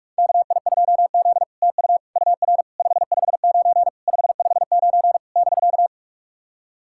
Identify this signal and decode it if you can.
Morse code, heard as a keyed tone, about 46 wpm, 700 Hz. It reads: KI2Z TU UR 559 559 <BK>
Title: SOTA QSO Practice 45wpm